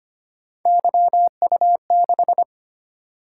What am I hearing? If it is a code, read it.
Morse code, heard as a keyed tone, about 25 wpm, 700 Hz. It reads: YU6
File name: Three Letter-Number Combinations - 25wpm